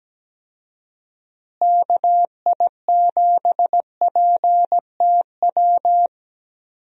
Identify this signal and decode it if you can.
Morse code, heard as a keyed tone, about 17 wpm, 700 Hz. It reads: KI7PTW
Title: USA Call Signs - 6 Characters - Repeated 2x - 17wpm